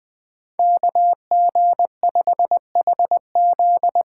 Morse code, heard as a keyed tone, about 20 wpm, 700 Hz. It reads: KG5HZ
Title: USA Call Signs - 8x Character Spacing - 20wpm